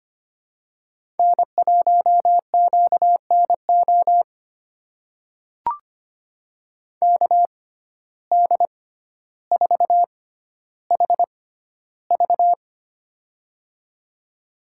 Morse code, heard as a keyed tone, about 25 wpm, spaced as for 13 wpm, 700 Hz. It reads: N1QNO KD4HV
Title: USA Call Signs - 5 Characters - 6x Character Spacing - 25wpm